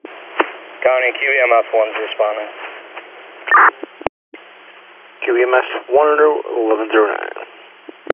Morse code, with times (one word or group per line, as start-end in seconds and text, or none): none